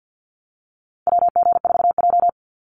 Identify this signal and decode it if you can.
Morse code, heard as a keyed tone, about 50 wpm, 700 Hz. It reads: WG4J
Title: USA Call Signs 50wpm